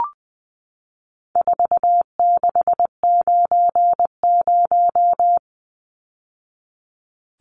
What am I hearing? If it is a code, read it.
Morse code, heard as a keyed tone, about 20 wpm, 700 Hz. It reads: 4690
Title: Numbers - 4-Digit 20wpm